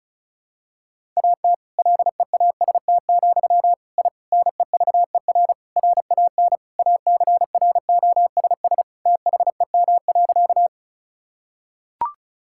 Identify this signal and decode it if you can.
Morse code, heard as a keyed tone, about 34 wpm, 700 Hz. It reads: AT LEAST, I NEVER RAN ACROSS THEM.
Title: Sentences from Top 1000 Words - Review Part 5 - 35wpm